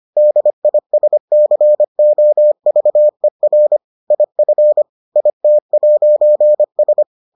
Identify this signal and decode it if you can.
Morse code, heard as a keyed tone, about 25 wpm, 600 Hz. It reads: DISCOVER IF IT'S